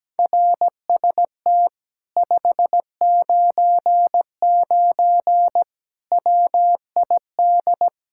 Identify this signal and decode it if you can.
Morse code, heard as a keyed tone, about 17 wpm, 700 Hz. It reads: RST 599 WID
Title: Speed-Racing Common QSO Phrases Encore - 17wpm